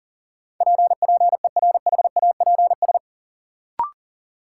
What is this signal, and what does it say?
Morse code, heard as a keyed tone, about 40 wpm, 700 Hz. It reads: P PERHAPS
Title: Sentences from Top 1100 Words - New Word Drill - 40wpm